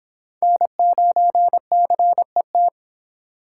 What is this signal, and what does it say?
Morse code, heard as a keyed tone, about 29 wpm, 700 Hz. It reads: N9CET